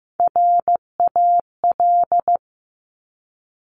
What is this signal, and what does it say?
Morse code, heard as a keyed tone, about 15 wpm, 700 Hz. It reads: RAL
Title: Top Three Letter Combinations in Words 15wpm